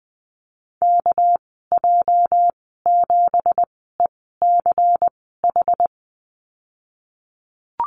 Morse code, heard as a keyed tone, about 20 wpm, spaced as for 15 wpm, 700 Hz. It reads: KJ7ECH
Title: 213 Lesson 34g - Number 8 - Call Signs 20wpm